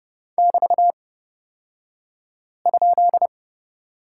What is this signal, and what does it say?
Morse code, heard as a keyed tone, about 30 wpm, spaced as for 5 wpm, 700 Hz. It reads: <BT>?
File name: Lesson 36d - BT - Two Characters 30wpm